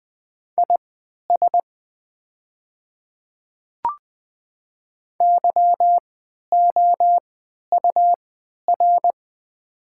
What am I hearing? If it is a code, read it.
Morse code, heard as a keyed tone, about 20 wpm, spaced as for 11 wpm, 700 Hz. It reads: IS YOUR
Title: 3x Character Spacing - Top 200 Words - 20wpm